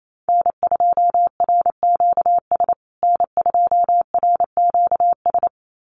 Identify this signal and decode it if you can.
Morse code, heard as a keyed tone, about 28 wpm, 700 Hz. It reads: N2RQS N2RQS
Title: USA Call Signs Repeated 2x 28wpm